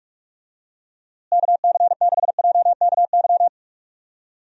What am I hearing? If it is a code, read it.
Morse code, heard as a keyed tone, about 43 wpm, 700 Hz. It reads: KC6JKY
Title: USA Call Signs - 6 Characters - Repeated 2x - 45wpm